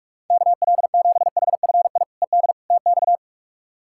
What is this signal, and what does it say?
Morse code, heard as a keyed tone, about 41 wpm, 700 Hz. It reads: KL7HFI ED TX
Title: K1USN SST QSO Practice - 45wpm